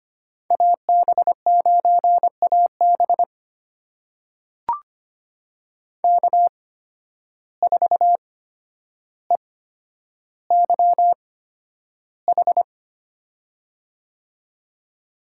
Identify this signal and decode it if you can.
Morse code, heard as a keyed tone, about 25 wpm, spaced as for 11 wpm, 700 Hz. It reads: AB9AB K4EYH
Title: USA Call Signs - 5 Characters - 8x Character Spacing - 25wpm